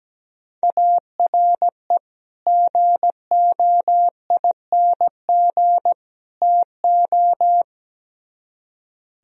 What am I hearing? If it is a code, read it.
Morse code, heard as a keyed tone, about 17 wpm, 700 Hz. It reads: ARE GOING TO
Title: Speed-Racing - Sets of 3 Words 17wpm